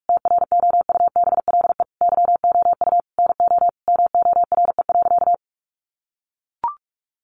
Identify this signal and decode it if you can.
Morse code, heard as a keyed tone, about 44 wpm, 700 Hz. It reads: TROUBLE YOU NO MORE.
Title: Sentences from Top 1200 Words 45wpm